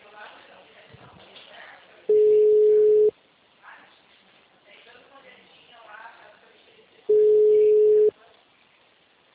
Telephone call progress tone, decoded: ringback tone

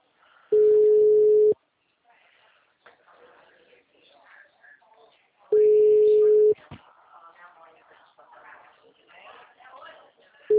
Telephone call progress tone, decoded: ringback tone

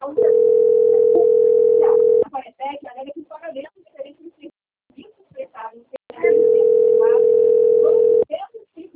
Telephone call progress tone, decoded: ringback tone

